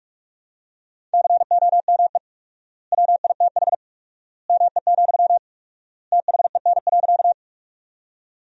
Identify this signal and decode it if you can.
Morse code, heard as a keyed tone, about 46 wpm, 700 Hz. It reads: COME WITH ME, THEN.
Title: 4x Word-Spacing - Sentences from Top 200 Words - 45wpm